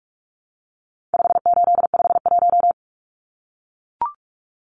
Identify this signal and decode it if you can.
Morse code, heard as a keyed tone, about 44 wpm, 700 Hz. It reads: 5851
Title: Numbers - 4-Digit 45wpm